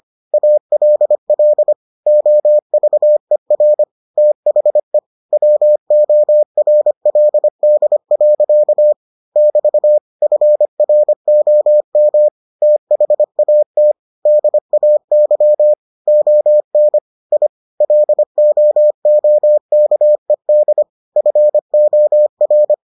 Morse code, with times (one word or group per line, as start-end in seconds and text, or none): 0.3-1.7 ALL
2.1-3.8 OVER
4.2-5.0 THE
5.3-8.9 WORLD.
9.4-10.0 <BT>
10.2-12.3 FROM
12.6-13.9 THAT
14.3-15.7 DAY
16.1-17.0 ON
17.3-17.5 I
17.8-20.8 LOOKED
21.2-22.7 FOR